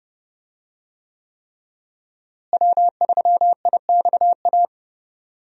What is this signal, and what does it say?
Morse code, heard as a keyed tone, about 32 wpm, 700 Hz. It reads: W3IXA
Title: USA Call Signs - 5 Characters - Repeated 3x - 30wpm